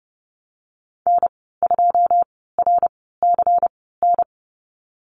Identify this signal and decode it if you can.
Morse code, heard as a keyed tone, about 30 wpm, spaced as for 17 wpm, 700 Hz. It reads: N2RCN
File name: Lesson 19g - Letter w - Call Signs 30wpm